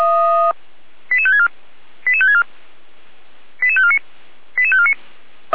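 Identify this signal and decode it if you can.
Five-tone selective call (ZVEI), heard as quick runs of five tones, tones of about 70 ms. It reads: 88564 88564 88549 88549
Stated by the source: ZVEI five-tone selective call 88417 received by SDR, recorded by Feuerwehr Orlamünde.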